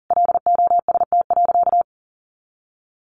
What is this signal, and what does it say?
Morse code, heard as a keyed tone, about 40 wpm, 700 Hz. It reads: LOST.
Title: Sentences from Top 1400 Words 40wpm